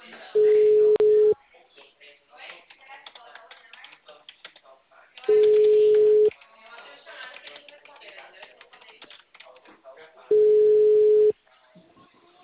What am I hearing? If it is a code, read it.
Ringback tone (Europe, 425 Hz), with speech.
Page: Ribeirao sala 906